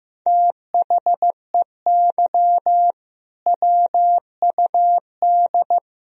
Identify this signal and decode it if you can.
Morse code, heard as a keyed tone, about 15 wpm, 700 Hz. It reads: THEY WUD
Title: Sentences from Top 500 Words and Abbreviations 15wpm